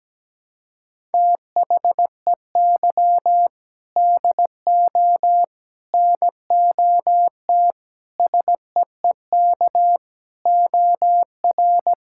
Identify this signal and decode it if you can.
Morse code, heard as a keyed tone, about 17 wpm, 700 Hz. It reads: THEY DO NOT SEEK OR
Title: Sentences from Top 1600 Words 17wpm